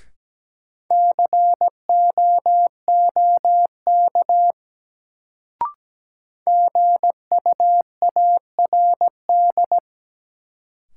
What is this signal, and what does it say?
Morse code, heard as a keyed tone, about 17 wpm, 700 Hz. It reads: COOK GUARD
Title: Sentences from Top 700 Words - New Word Drill - 17wpm